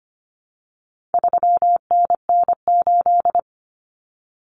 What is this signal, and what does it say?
Morse code, heard as a keyed tone, about 25 wpm, 700 Hz. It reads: 3NN8
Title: Common Cut Numbers - 4-Digit 25wpm